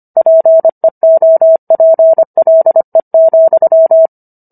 Morse code, heard as a keyed tone, about 25 wpm, 650 Hz. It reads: PEOPLE,